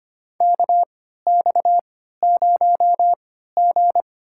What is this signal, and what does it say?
Morse code, heard as a keyed tone, about 25 wpm, spaced as for 14 wpm, 700 Hz. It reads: KX0G
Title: USA Call Signs - 4 Characters - 3x Character Spacing - 25wpm